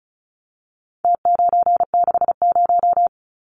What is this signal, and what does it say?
Morse code, heard as a keyed tone, about 34 wpm, 700 Hz. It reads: T960
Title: Numbers - Contest Style - 4-Digit 35wpm